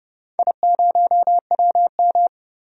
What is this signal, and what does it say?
Morse code, heard as a keyed tone, about 29 wpm, 700 Hz. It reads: I0WM
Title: DXCC-100 Call Signs - 4 Characters - 30wpm